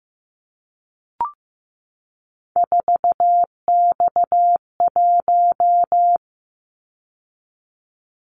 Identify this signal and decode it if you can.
Morse code, heard as a keyed tone, about 15 wpm, 700 Hz. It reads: 4X1
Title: Callsign Prefixes for Top 50 DXCC - 15wpm